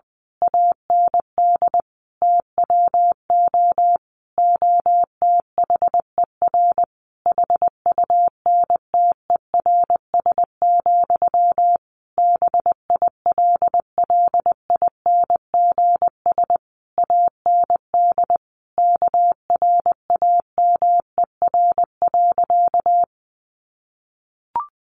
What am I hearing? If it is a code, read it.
Morse code, heard as a keyed tone, about 20 wpm, 700 Hz. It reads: AND TWO OTHER HUNTERS, BILLINGS AND KRAMER.
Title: A Sound Of Thunder 20wpm